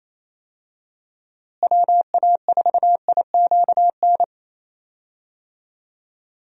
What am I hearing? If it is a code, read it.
Morse code, heard as a keyed tone, about 28 wpm, 700 Hz. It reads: WA4IQN
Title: USA Call Signs - 6 Characters - Repeated 3x - 28wpm